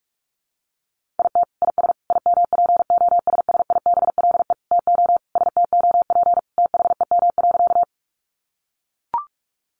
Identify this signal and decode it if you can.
Morse code, heard as a keyed tone, about 44 wpm, 700 Hz. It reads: IT IS IMPOSSIBLE TO STOP THEM.
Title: Sentences from Top 1100 Words 45wpm